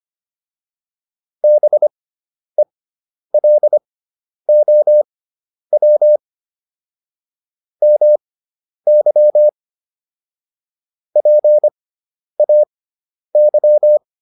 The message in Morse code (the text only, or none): BELOW MY PAY